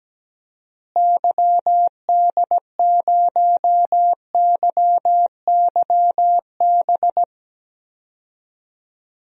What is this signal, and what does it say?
Morse code, heard as a keyed tone, about 17 wpm, 700 Hz. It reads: YD0YYB
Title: DXCC-100 Call Signs - 6 Characters - Repeated 3x - 17wpm